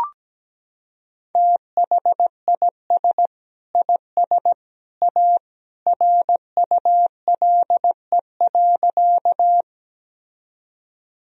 Morse code, heard as a keyed tone, about 17 wpm, 700 Hz. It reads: THIS IS A RULE.
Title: Sentences from Top 1000 Words - Review Part 2 - 17wpm